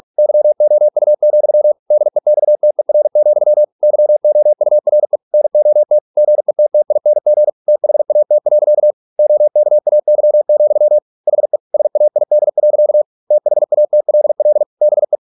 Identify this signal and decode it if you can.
Morse code, heard as a keyed tone, about 46 wpm, 600 Hz. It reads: YOU, DEXTER, YOURE NOT GETTING THAT. OKAY, HE SAID. THATLL BE